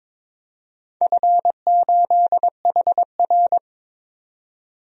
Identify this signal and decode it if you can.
Morse code, heard as a keyed tone, about 21 wpm, 690 Hz. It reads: F8HR
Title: DXCC-100 Call Signs - 4 Characters - 22wpm